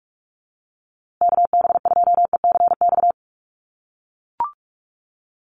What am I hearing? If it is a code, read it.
Morse code, heard as a keyed tone, about 44 wpm, 700 Hz. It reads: KB2ECX
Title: USA Call Signs Repeated 3x 45wpm